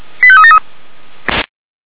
Five-tone selective call (ZVEI), heard as quick runs of five tones, tones of about 70 ms. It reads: 85373
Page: SDR/ZVEI